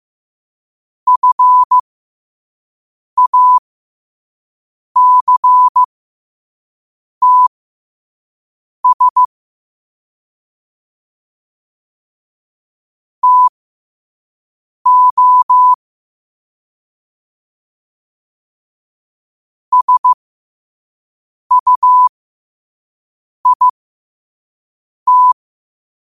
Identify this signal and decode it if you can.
Morse code, heard as a keyed tone, about 15 wpm, spaced as for 5 wpm, 1000 Hz. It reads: FACTS TO SUIT
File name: qotdfiles / 5wpm